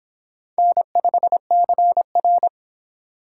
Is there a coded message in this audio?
Yes: Morse code from 0.6 to 2.5 s, heard as a keyed tone, about 26 wpm, 700 Hz. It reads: N5CR